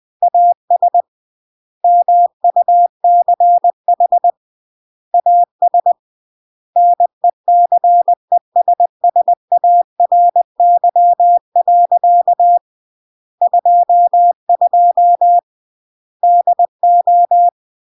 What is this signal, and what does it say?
Morse code, heard as a keyed tone, about 20 wpm, 700 Hz. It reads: AS MUCH AS NECESSARY. 22 DO
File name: CW / jodorowsky / 82 commandments 20-20